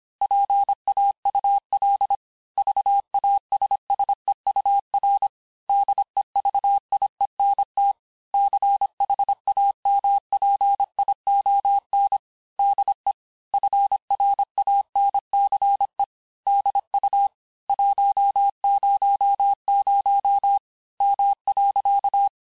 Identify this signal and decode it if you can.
Morse code, heard as a keyed tone, about 26 wpm, 800 Hz. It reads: PAUL VASSEUR DEVIENT CHAMPION DE FRANCE DU 100 M.
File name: cours-du-lundi files / cours-1584-teste-clair-vitesse-19mpm-puis-24mpm